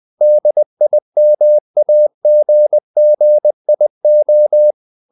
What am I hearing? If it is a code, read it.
Morse code, heard as a keyed tone, about 20 wpm, 600 Hz. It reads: DIMAGGIO
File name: the old man and the sea / the old man and the sea part 1 20wpm